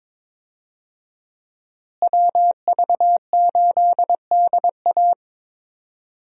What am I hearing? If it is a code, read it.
Morse code, heard as a keyed tone, about 22 wpm, 700 Hz. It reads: WV8DA
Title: USA Call Signs - 5 Characters - Repeated 3x - 22wpm